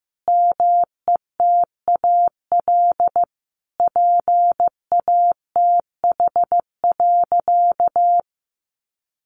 Morse code, heard as a keyed tone, about 15 wpm, 700 Hz. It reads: METAL PATH.
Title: A Sound Of Thunder 15wpm